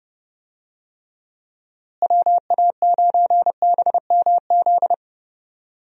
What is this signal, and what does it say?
Morse code, heard as a keyed tone, about 30 wpm, 700 Hz. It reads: WA9BMZ
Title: USA Call Signs - 6 Characters - Repeated 2x - 30wpm